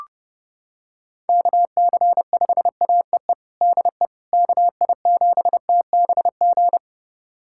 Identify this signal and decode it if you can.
Morse code, heard as a keyed tone, about 30 wpm, 700 Hz. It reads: KC5AEE DE KI7TBG
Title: USA Callsign DE Callsign - 6 Characters - 30wpm